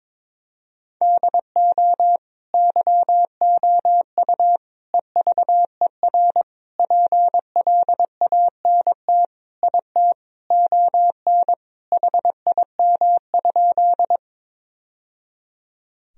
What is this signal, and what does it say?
Morse code, heard as a keyed tone, about 22 wpm, 700 Hz. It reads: DO YOU EVER PLANT IT ON HIM?
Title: Sentences from Top 1000 Words - Review Part 1 - 22wpm